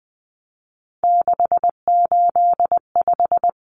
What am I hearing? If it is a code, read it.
Morse code, heard as a keyed tone, about 20 wpm, 700 Hz. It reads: 685
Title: Numbers - 3-Digit 20wpm